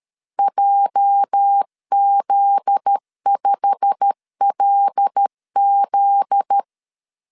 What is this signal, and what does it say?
Morse code, heard as a keyed tone, about 13 wpm, 800 Hz. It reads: JZ5LZ
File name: LEZIONI / 9 2 12wpm 2301